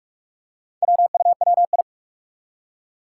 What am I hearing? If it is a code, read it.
Morse code, heard as a keyed tone, about 44 wpm, 700 Hz. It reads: WUWI
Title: Four Challenging Letter-Number Combinations 45wpm